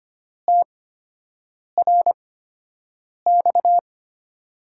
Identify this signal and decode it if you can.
Morse code, heard as a keyed tone, about 25 wpm, spaced as for 7 wpm, 700 Hz. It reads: TRX
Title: USA Call Signs - 5 Characters - 8x Character Spacing - 25wpm